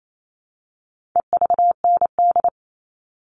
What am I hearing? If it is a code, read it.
Morse code, heard as a keyed tone, about 28 wpm, 700 Hz. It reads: EVND
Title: All Cut Numbers - 4-Digit 28wpm